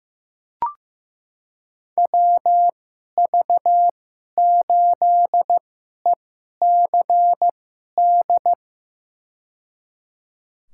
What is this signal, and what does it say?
Morse code, heard as a keyed tone, about 15 wpm, spaced as for 11 wpm, 700 Hz. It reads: WV8ECD
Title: USA Call Signs - 2x Character Spacing - 15wpm